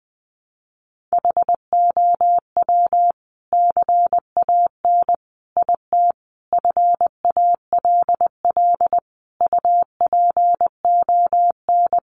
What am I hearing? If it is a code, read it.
Morse code, heard as a keyed tone, about 20 wpm, 700 Hz. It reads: HOW CAN IT FALL UPON
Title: Sentences from Top 1100 Words 20wpm